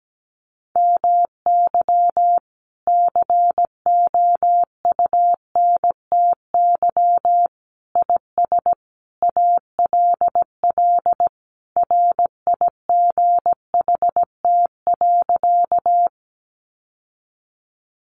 Morse code, heard as a keyed tone, about 17 wpm, 700 Hz. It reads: MY COUNTY IS ALL RIGHT.
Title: Sentences from Top 1200 Words 17wpm